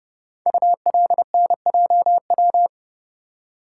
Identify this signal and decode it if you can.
Morse code, heard as a keyed tone, about 30 wpm, 700 Hz. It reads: ULNJW
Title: Code Groups - 5 Letters - 30wpm